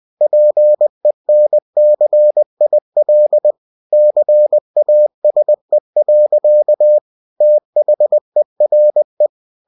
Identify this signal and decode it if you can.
Morse code, heard as a keyed tone, about 20 wpm, 600 Hz. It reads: PENCIL CASE. THERE
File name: cw-books / winnie-the-pooh / winnie-20wpm-page-0109